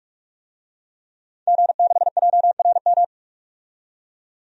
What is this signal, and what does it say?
Morse code, heard as a keyed tone, about 47 wpm, 700 Hz. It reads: G6JRM
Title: DXCC-100 Call Signs - 5 Characters - 45wpm